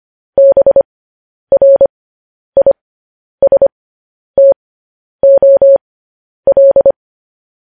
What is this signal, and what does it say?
Morse code, heard as a keyed tone, about 25 wpm, spaced as for 10 wpm, 550 Hz. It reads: BRISTOL
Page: News Headlines in Morse Code at 10 WPM